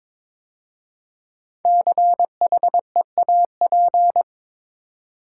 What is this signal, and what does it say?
Morse code, heard as a keyed tone, about 22 wpm, 700 Hz. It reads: CHEAP
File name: 5 Letter Words - First 3 Letters Given - 22wpm